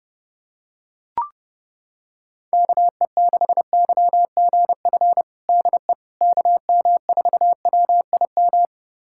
Morse code, heard as a keyed tone, about 32 wpm, 700 Hz. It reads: KE6YGF DE KM4WIM
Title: USA Callsign DE Callsign - 6 Characters - 30wpm